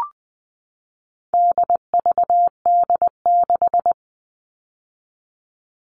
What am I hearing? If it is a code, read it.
Morse code, heard as a keyed tone, about 20 wpm, 700 Hz. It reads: DVD6
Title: All Cut Numbers - 4-Digit 20wpm